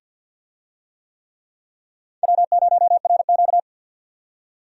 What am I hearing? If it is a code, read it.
Morse code, heard as a keyed tone, about 51 wpm, 700 Hz. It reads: W0RQ
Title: USA Call Signs - 4 Characters - 50wpm